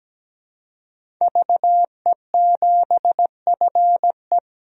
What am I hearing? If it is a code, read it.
Morse code, heard as a keyed tone, about 17 wpm, 700 Hz. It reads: VE7FE
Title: DXCC-100 Call Signs - 5 Characters - 17wpm